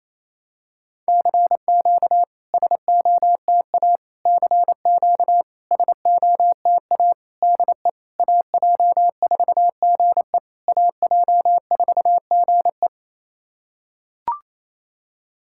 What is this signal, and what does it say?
Morse code, heard as a keyed tone, about 28 wpm, 690 Hz. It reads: CQ SOTA CQ SOTA DE AJ4GE AJ4GE
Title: SOTA QSO Practice 28wpm